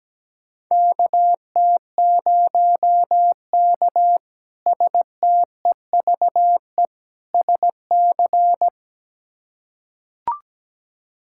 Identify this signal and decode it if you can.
Morse code, heard as a keyed tone, about 17 wpm, 700 Hz. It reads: KT0K STEVE SC
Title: K1USN SST QSO Practice - 17wpm